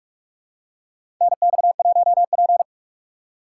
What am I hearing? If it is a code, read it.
Morse code, heard as a keyed tone, about 46 wpm, 700 Hz. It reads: NX1P
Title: USA Call Signs - 4 Characters - 8x Character Spacing - 45wpm